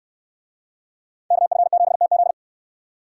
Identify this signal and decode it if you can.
Morse code, heard as a keyed tone, about 66 wpm, 700 Hz. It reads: KF7TZ